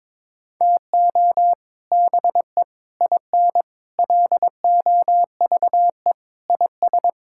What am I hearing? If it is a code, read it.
Morse code, heard as a keyed tone, about 22 wpm, 700 Hz. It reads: TO BE IN LOVE IS
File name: practice / 22wpm / Sentences with 3-Word Phrases - 22wpm